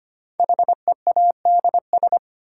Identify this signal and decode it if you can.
Morse code, heard as a keyed tone, about 25 wpm, 700 Hz. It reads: HEADS
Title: Sentences from Top 1000 Words - Review Part 3 - New Word Drill - 25wpm